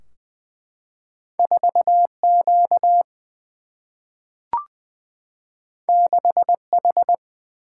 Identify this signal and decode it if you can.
Morse code, heard as a keyed tone, about 20 wpm, 700 Hz. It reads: 4 Q 6 H
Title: Single-Letter Single-Number 20wpm